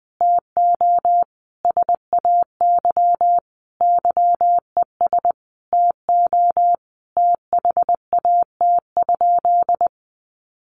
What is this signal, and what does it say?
Morse code, heard as a keyed tone, about 20 wpm, 700 Hz. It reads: TO SAY YES TO THAT?
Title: Sentences from Top 1100 Words 20wpm